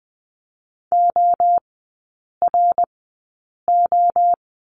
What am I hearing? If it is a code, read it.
Morse code, heard as a keyed tone, about 20 wpm, spaced as for 8 wpm, 700 Hz. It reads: ORO
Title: 060 Lesson 11e - Letter h - Three Characters 20wpm